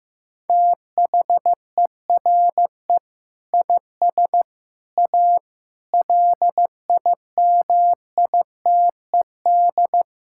Morse code, heard as a keyed tone, about 15 wpm, 700 Hz. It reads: THERE IS A LIMITED
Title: Sentences from Top 1000 Words - Review Part 6 - 15wpm